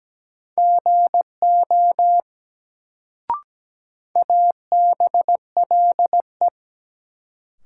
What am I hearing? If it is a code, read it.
Morse code, heard as a keyed tone, about 17 wpm, 700 Hz. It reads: GO ABLE
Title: Sets of 2 Words - New Word Drill - 17wpm